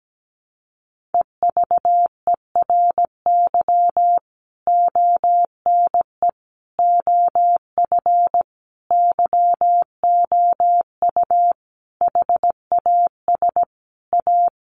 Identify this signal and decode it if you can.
Morse code, heard as a keyed tone, about 17 wpm, 700 Hz. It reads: EVERY ONE OF YOU HAS A
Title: Sentences from Top 1500 Words 17wpm